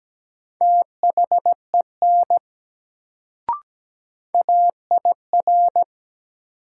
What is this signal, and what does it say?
Morse code, heard as a keyed tone, about 17 wpm, 700 Hz. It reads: THEN AIR
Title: Sets of 2 Words - New Word Drill - 17wpm